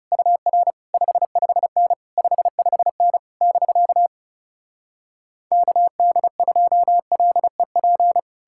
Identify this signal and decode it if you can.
Morse code, heard as a keyed tone, about 40 wpm, 700 Hz. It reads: UR 55N 55N <BK> KD2LEP